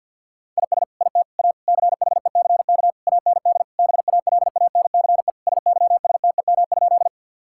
Morse code, heard as a keyed tone, about 50 wpm, 700 Hz. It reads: IS IT A CHECK AND BALANCE SYSTEM?
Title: Sentences from Top 1000 Words - Review Part 6 - 50wpm